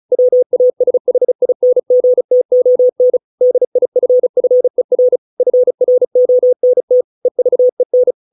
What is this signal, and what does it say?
Morse code, heard as a keyed tone, about 35 wpm, 500 Hz. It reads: WASHINGTON DIFFER FRONT EVEN